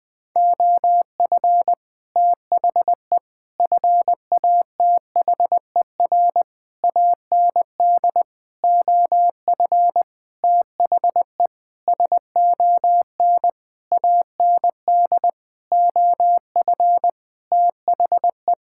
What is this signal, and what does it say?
Morse code, heard as a keyed tone, about 20 wpm, 700 Hz. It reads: OF THE FATHER AND OF THE SON AND OF THE
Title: Sentences with 9-Word Phrases - Spoken Beginning - 20wpm